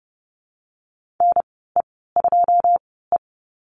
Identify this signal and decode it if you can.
Morse code, heard as a keyed tone, about 30 wpm, spaced as for 17 wpm, 700 Hz. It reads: NE2E
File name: Lesson 15g - Number 5 - Call Signs 30wpm